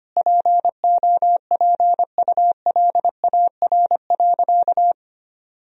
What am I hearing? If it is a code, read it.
Morse code, heard as a keyed tone, about 25 wpm, 700 Hz. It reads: POPULAR.
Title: Sentences from Top 1000 Words - Review Part 3 - 25wpm